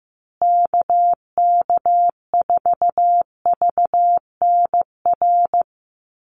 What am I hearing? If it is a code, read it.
Morse code, heard as a keyed tone, about 15 wpm, 700 Hz. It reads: KK4VNR
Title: USA Call Signs Repeated 2x 15wpm